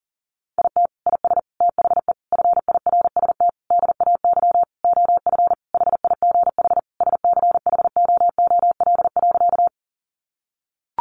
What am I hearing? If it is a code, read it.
Morse code, heard as a keyed tone, about 40 wpm, 700 Hz. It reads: IT IS THE FIRST DAY OF HIGH SCHOOL.
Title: Additional Easy Phrases 40wpm